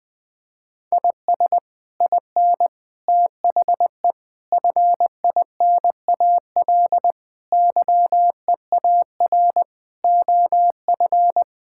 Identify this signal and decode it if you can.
Morse code, heard as a keyed tone, about 20 wpm, 700 Hz. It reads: IS IN THE FINAL YEAR OF
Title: Sets of 8 Words - 20wpm